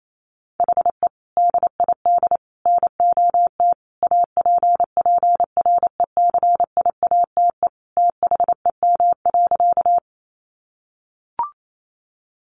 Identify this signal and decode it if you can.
Morse code, heard as a keyed tone, about 28 wpm, 700 Hz. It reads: HE DID NOT APPRECIATE THEM.
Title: Sentences from Top 1300 Words 28wpm